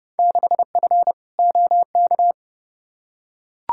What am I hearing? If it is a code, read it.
Morse code, heard as a keyed tone, about 30 wpm, 700 Hz. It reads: 6F OK
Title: Field Day Exchanges Encore - 30wpm